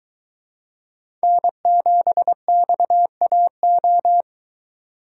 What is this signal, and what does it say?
Morse code, heard as a keyed tone, about 22 wpm, 700 Hz. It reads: N7XAO